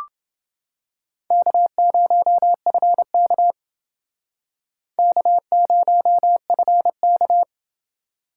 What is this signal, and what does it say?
Morse code, heard as a keyed tone, about 29 wpm, 700 Hz. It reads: K0FK K0FK